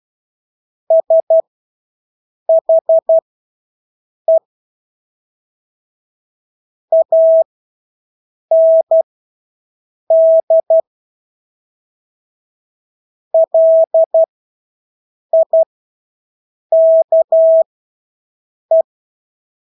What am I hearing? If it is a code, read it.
Morse code, heard as a keyed tone, about 12 wpm, spaced as for 6 wpm, 650 Hz. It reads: SHE AND LIKE
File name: Engelske ord 12-6 wpm